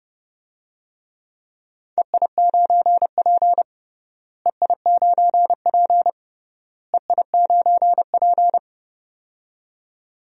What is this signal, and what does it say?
Morse code, heard as a keyed tone, about 30 wpm, 700 Hz. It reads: EI9P EI9P EI9P
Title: DXCC-100 Call Signs - 4 Characters - Repeated 3x - 30wpm